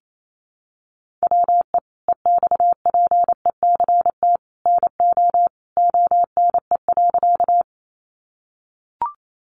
Morse code, heard as a keyed tone, about 28 wpm, 700 Hz. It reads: WE EXPECT NO ONE.
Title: Sentences from Top 1100 Words 28wpm